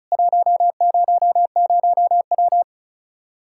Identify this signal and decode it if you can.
Morse code, heard as a keyed tone, about 34 wpm, 700 Hz. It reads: 100W
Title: Practice QSO - Exchange 2 - Info - 35wpm